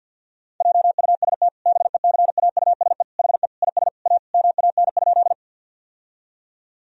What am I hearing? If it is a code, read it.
Morse code, heard as a keyed tone, about 49 wpm, 700 Hz. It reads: JUST BECAUSE HE IS A MAN?
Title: Sentences from Top 1000 Words - Review Part 3 - 50wpm